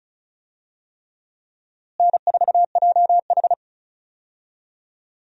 Morse code, heard as a keyed tone, about 34 wpm, 700 Hz. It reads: N4JH
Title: USA Call Signs - 4 Characters - Repeated 3x - 35wpm